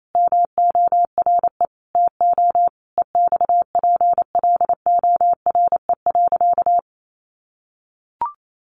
Morse code, heard as a keyed tone, about 28 wpm, 700 Hz. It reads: MORE TO EXPLORE.
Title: Sentences from Top 1100 Words 28wpm